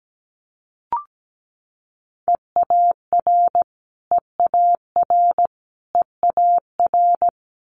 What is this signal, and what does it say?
Morse code, heard as a keyed tone, about 17 wpm, 700 Hz. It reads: EAR EAR EAR
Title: Multi-letter Phonemes Repeated 3x 17wpm